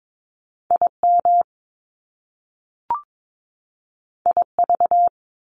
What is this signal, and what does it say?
Morse code, heard as a keyed tone, about 22 wpm, 700 Hz. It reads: IM IV
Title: Top Two Letter Combinations in Words 22wpm